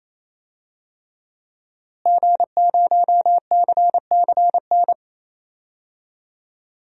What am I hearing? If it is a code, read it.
Morse code, heard as a keyed tone, about 27 wpm, 700 Hz. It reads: G0CCN
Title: DXCC-100 Call Signs - 5 Characters - Repeated 3x - 28wpm